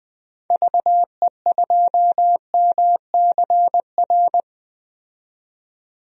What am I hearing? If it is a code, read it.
Morse code, heard as a keyed tone, about 20 wpm, 700 Hz. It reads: VE2MCR
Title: DXCC-100 Call Signs - 6 Characters - 20wpm